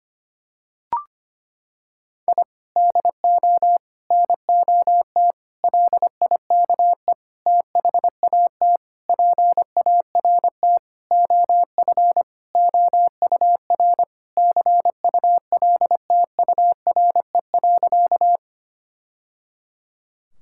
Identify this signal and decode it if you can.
Morse code, heard as a keyed tone, about 25 wpm, 700 Hz. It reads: I DO NOT LIKE THAT PART OF OUR CULTURE.
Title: Sentences from Top 1000 Words - Review Part 6 - 25wpm